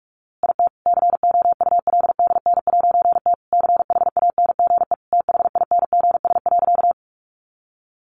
Morse code, heard as a keyed tone, about 44 wpm, 700 Hz. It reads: IT COULDN'T CHANGE THINGS.
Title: A Sound Of Thunder 45wpm